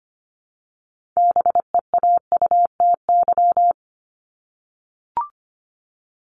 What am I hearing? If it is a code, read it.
Morse code, heard as a keyed tone, about 25 wpm, 700 Hz. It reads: BEAUTY
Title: Top 500 Words Repeated 25wpm